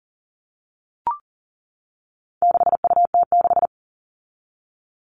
Morse code, heard as a keyed tone, about 40 wpm, 700 Hz. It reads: 6UT6
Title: All Cut Numbers - 4-Digit 40wpm